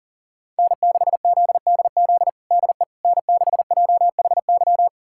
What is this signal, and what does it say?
Morse code, heard as a keyed tone, about 40 wpm, 700 Hz. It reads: N6ZDZ DE N6JHY
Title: USA Callsign DE Callsign - 5 Characters - 40wpm